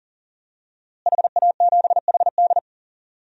Morse code, heard as a keyed tone, about 40 wpm, 700 Hz. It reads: HA7HD
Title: DXCC-100 Call Signs - 5 Characters - Repeated 3x - 40wpm